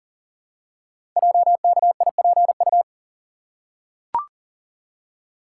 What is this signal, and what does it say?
Morse code, heard as a keyed tone, about 42 wpm, 690 Hz. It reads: JKIPU
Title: Code Groups - 5 Letters - 40wpm